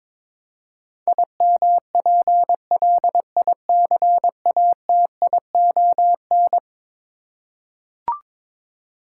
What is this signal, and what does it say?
Morse code, heard as a keyed tone, about 22 wpm, 700 Hz. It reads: IMPLICATION
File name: practice / 22wpm / Eleven Letter Words - 22wpm